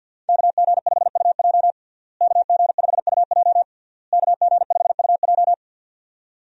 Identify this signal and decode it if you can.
Morse code, heard as a keyed tone, about 50 wpm, 700 Hz. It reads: KG5UJ KG5UJ KG5UJ
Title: USA Call Signs - 5 Characters - Repeated 3x - 50wpm